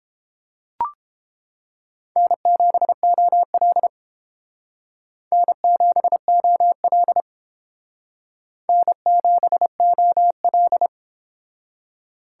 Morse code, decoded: N7OL N7OL N7OL

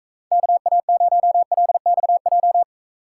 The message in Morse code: KA0LXJ

